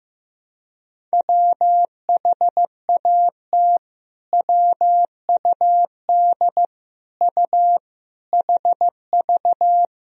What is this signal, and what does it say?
Morse code, heard as a keyed tone, about 15 wpm, 700 Hz. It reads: WHAT WUD U HV
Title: Sentences from Top 200 Words and Abbreviations 15wpm